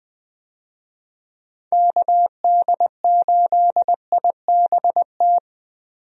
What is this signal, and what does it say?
Morse code, heard as a keyed tone, about 20 wpm, 700 Hz. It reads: KD8IBT
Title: USA Call Signs - 6 Characters - 20wpm